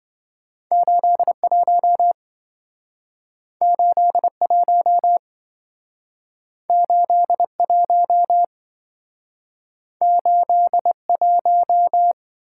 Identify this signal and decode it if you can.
Morse code, heard as a keyed tone, about 27 wpm, 700 Hz. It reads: 81 81 81 81